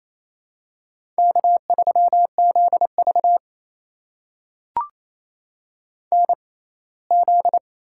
Morse code, heard as a keyed tone, about 27 wpm, spaced as for 19 wpm, 700 Hz. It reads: K3ZV NZ
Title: USA Call Signs - 4 Characters - 6x Character Spacing - 28wpm